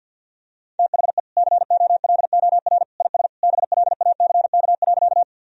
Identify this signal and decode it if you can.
Morse code, heard as a keyed tone, about 51 wpm, 700 Hz. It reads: THE COLOR IS BLACK.
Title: Sentences from Top 1000 Words - Review Part 2 - 50wpm